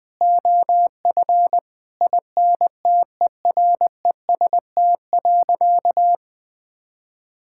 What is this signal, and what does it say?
Morse code, heard as a keyed tone, about 20 wpm, 700 Hz. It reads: OF INTEREST.
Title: Sentences from Top 1000 Words - Review Part 5 - 20wpm